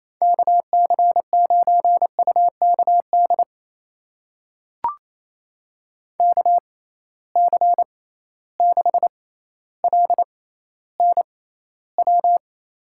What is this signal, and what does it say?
Morse code, heard as a keyed tone, about 28 wpm, spaced as for 14 wpm, 700 Hz. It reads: KC9UKD KC6LNW
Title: USA Call Signs - 6 Characters - 6x Character Spacing - 28wpm